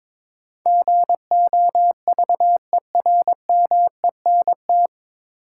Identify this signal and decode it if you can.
Morse code, heard as a keyed tone, about 23 wpm, 700 Hz. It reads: GOVERMENT
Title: Sentences from Top 1300 Words - New Word Drill - 22wpm